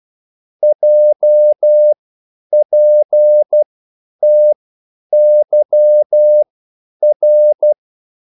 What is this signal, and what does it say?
Morse code, heard as a keyed tone, about 12 wpm, spaced as for 9 wpm, 600 Hz. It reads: JPTYR
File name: QSL / CW lesson